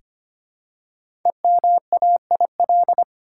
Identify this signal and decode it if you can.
Morse code, heard as a keyed tone, about 25 wpm, 700 Hz. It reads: EMAIL
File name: 5 Letter Words - First 3 Letters Given - 25wpm